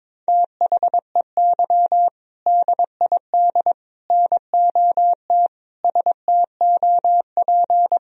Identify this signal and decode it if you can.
Morse code, heard as a keyed tone, about 22 wpm, 700 Hz. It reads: THEY DID NOT STOP
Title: Sentences from Top 1000 Words - Review Part 2 - 22wpm